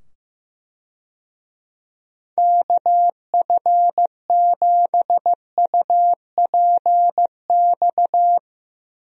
Morse code, heard as a keyed tone, about 15 wpm, 700 Hz. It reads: KF7UPX
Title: USA Call Signs - 6 Characters - 15wpm